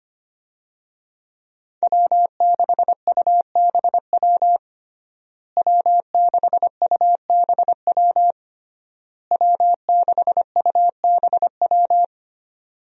Morse code, heard as a keyed tone, about 25 wpm, 700 Hz. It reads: W6UBW W6UBW W6UBW
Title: USA Call Signs - 5 Characters - Repeated 3x - 25wpm